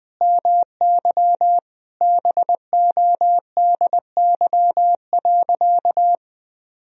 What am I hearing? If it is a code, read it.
Morse code, heard as a keyed tone, about 20 wpm, 700 Hz. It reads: MY BODY.
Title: Sentences from Top 1000 Words - Review Part 1 - 20wpm